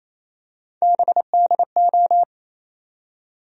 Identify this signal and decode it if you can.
Morse code, heard as a keyed tone, about 28 wpm, 700 Hz. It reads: BDO
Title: USA Call Sign Postfixes - 3 Letters 28wpm